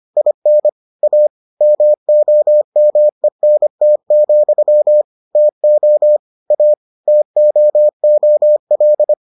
Morse code, heard as a keyed tone, about 25 wpm, 600 Hz. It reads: IN A MOMENT, TO A TOOL